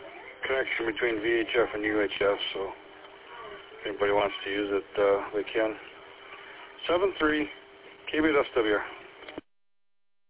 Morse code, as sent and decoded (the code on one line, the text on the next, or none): none